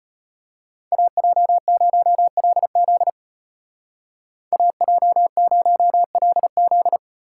Morse code, heard as a keyed tone, about 32 wpm, 700 Hz. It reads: AJ0LZ AJ0LZ